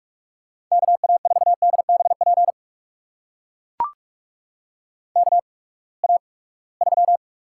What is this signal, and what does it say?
Morse code, heard as a keyed tone, about 44 wpm, spaced as for 25 wpm, 700 Hz. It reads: KA4DBP KA3
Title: USA Call Signs - 6 Characters - 8x Character Spacing - 45wpm